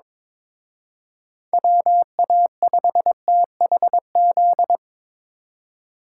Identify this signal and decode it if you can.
Morse code, heard as a keyed tone, about 25 wpm, 700 Hz. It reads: WA5THZ
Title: Speed Racing - USA 2x3 Call Signs 22wpm